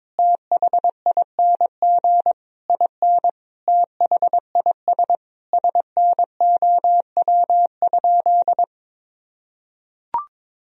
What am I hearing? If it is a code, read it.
Morse code, heard as a keyed tone, about 22 wpm, 700 Hz. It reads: THING IN THIS SNOW?
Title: Sentences from Top 1000 Words - Review Part 3 - 22wpm